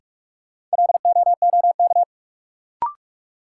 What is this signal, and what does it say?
Morse code, heard as a keyed tone, about 46 wpm, 700 Hz. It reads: LOOK
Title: Sentences from Top 100 Words - New Word Drill - 45wpm